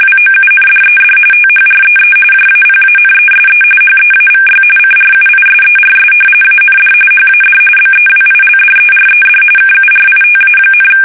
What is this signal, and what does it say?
RTTY, about 45.45 baud. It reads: WOUNDED THREE JAPANESE HEAVY CRUISERS AND HAD BEWILDERED KURITA,
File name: fsk850 / battleshipnj